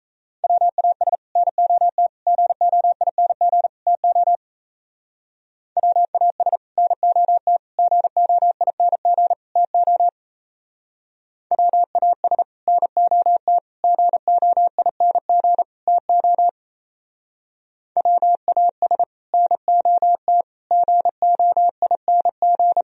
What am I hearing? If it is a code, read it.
Morse code, heard as a keyed tone, about 34 wpm, 700 Hz. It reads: WAS NOT GOING TO WAS NOT GOING TO WAS NOT GOING TO WAS NOT GOING
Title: Speed-Racing - Sets of 4 Words 28wpm